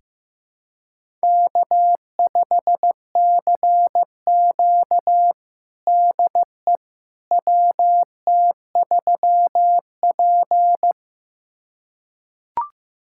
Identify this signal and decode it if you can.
Morse code, heard as a keyed tone, about 15 wpm, 700 Hz. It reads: K5CQ DE WT3P
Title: USA Callsign DE Callsign - 4 Characters - 15wpm